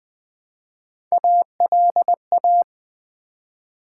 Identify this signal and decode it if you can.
Morse code, heard as a keyed tone, about 20 wpm, 690 Hz. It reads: ALA
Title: Build A Word - 1 Letter At A Time - 5-Letter Words - 20wpm